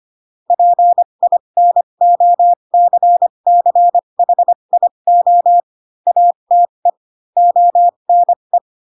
Morse code, heard as a keyed tone, about 25 wpm, spaced as for 22 wpm, 700 Hz. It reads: PINOCCHIO ATE ONE
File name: Pinocchio / Pinocchio ch7 18 25 ik1hge